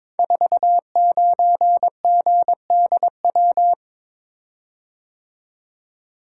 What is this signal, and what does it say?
Morse code, heard as a keyed tone, about 22 wpm, 700 Hz. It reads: 49GDW
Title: Code Groups - 5 Letter:Numbers - 22wpm